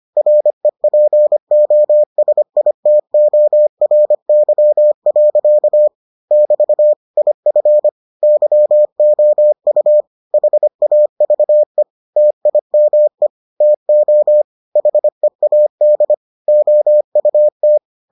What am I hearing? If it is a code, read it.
Morse code, heard as a keyed tone, about 25 wpm, 600 Hz. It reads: REPOSITORY. <BT> IF YOU HAVE TIME TO HEAD OUT